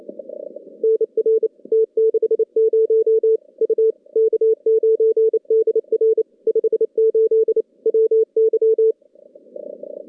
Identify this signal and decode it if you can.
Morse code, heard as a keyed tone, about 28 wpm, 410 Hz. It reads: NR T60 U K9DR 58 WY
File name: contests / CW-SS-k5tr- -2200